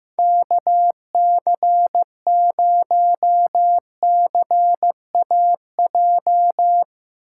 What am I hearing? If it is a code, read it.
Morse code, heard as a keyed tone, about 18 wpm, 700 Hz. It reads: KC0CAJ
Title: Speed Racing - USA 2x3 Call Signs 15wpm